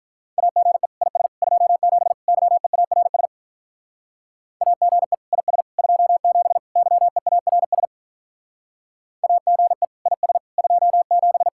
Morse code, heard as a keyed tone, about 44 wpm, 700 Hz. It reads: AGE IS 27 YEARS AGE IS 27 YEARS AGE IS 27